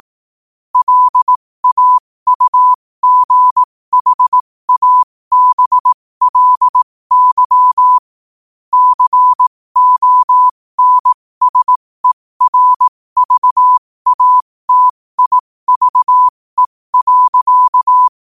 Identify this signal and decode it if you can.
Morse code, heard as a keyed tone, about 18 wpm, spaced as for 15 wpm, 1000 Hz. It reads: LAUGHABLY CONSERVATIVE.
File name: qotdfiles / 15wpm